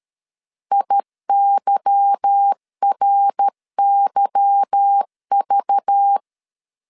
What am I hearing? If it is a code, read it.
Morse code, heard as a keyed tone, about 13 wpm, 800 Hz. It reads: IYRYV
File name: LEZIONI / 8 YJ45 / 8 Y 12wpm 2228